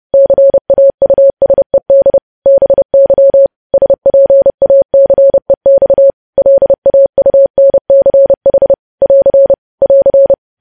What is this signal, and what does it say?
Morse code, heard as a keyed tone, about 30 wpm, 550 Hz. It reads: CAUSED BY SPACEX LAUNCH <AR> <AR>
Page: news headlines in morse code at 30 wpm